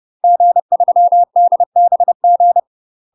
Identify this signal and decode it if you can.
Morse code, heard as a keyed tone, about 30 wpm, 700 Hz. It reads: G3DBG
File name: UK MASTER QSOs TRAINING / 30WPM / UK MASTER QSOs 0023